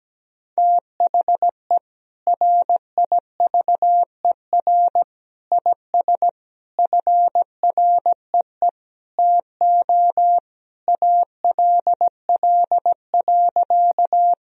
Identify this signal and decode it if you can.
Morse code, heard as a keyed tone, about 17 wpm, 700 Hz. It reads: THE RIVER IS FREE TO ALL.
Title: Sentences from Top 1000 Words - Review Part 1 - 17wpm